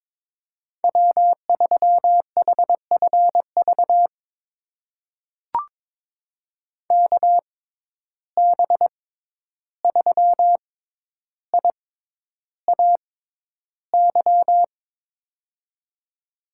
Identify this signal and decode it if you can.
Morse code, heard as a keyed tone, about 23 wpm, spaced as for 11 wpm, 700 Hz. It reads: W3HFV KB3IAY
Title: USA Call Signs - 6x Character Spacing - 22wpm